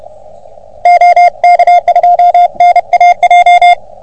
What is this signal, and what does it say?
Morse code, heard as a keyed tone, about 32 wpm, 720 Hz. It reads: OK2NAJ
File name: OM3PA PILUP CUC 388